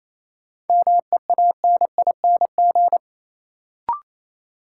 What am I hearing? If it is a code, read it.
Morse code, heard as a keyed tone, about 28 wpm, 700 Hz. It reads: MEANING
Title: Sentences from Top 900 Words - New Word Drill - 28wpm